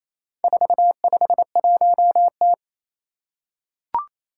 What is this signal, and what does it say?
Morse code, heard as a keyed tone, about 28 wpm, 700 Hz. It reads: 451T
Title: Four Challenging Letter-Number Combinations 28wpm